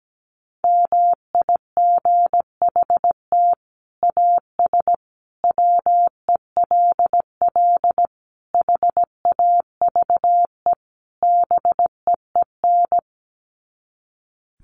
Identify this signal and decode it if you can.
Morse code, heard as a keyed tone, about 17 wpm, 700 Hz. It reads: MIGHT AS WELL HAVE BEEN
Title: Sets of 5 Words Encore2 17wpm